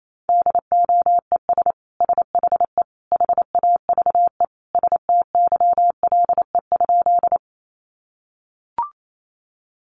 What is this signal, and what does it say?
Morse code, heard as a keyed tone, about 28 wpm, 700 Hz. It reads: DOES SHE HAVE STYLE?
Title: Sentences from Top 800 Words 28wpm